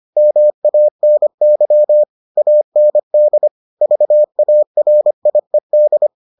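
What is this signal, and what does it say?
Morse code, heard as a keyed tone, about 25 wpm, 600 Hz. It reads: MANY AND VARIED
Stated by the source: Morse edition of the Foundations of Amateur Radio episodes.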